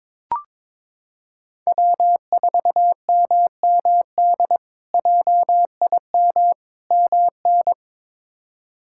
Morse code, heard as a keyed tone, about 22 wpm, 700 Hz. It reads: W4MMD JIM MN
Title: K1USN SST QSO Practice - 22wpm